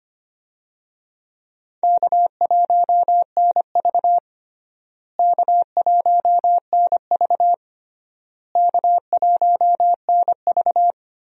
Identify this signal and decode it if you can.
Morse code, heard as a keyed tone, about 25 wpm, 700 Hz. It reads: K1NV K1NV K1NV
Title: USA Call Signs - 4 Characters - Repeated 3x - 25wpm